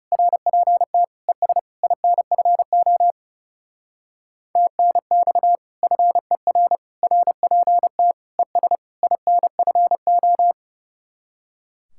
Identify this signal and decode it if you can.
Morse code, heard as a keyed tone, about 38 wpm, 700 Hz. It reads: RPT ES INFO TNX FER RPT ES INFO